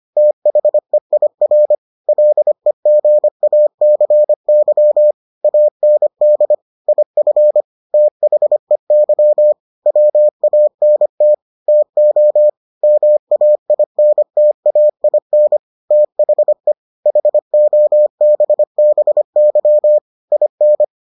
Morse code, heard as a keyed tone, about 25 wpm, 600 Hz. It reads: THEIR LEGACY AND IF THEY WANT TO MAINTAIN THE HOBBY IN